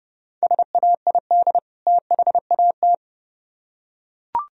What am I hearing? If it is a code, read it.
Morse code, heard as a keyed tone, about 29 wpm, 700 Hz. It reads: SAID THAT